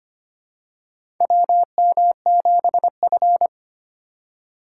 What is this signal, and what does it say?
Morse code, heard as a keyed tone, about 25 wpm, 700 Hz. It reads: WM7F
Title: USA Call Signs - 4 Characters - 3x Character Spacing - 25wpm